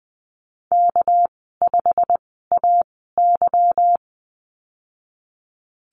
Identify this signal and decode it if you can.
Morse code, heard as a keyed tone, about 20 wpm, 700 Hz. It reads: K 5 A Y
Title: 238 Lesson 38g - Letter q - Call Signs 20wpm